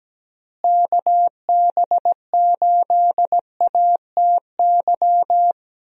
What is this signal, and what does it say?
Morse code, heard as a keyed tone, about 17 wpm, 700 Hz. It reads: KB8ATY
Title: USA Call Signs - 6 Characters - Repeated 2x - 17wpm